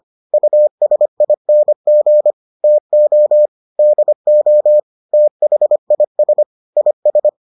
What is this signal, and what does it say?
Morse code, heard as a keyed tone, about 25 wpm, 600 Hz. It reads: USING TO DO THIS IS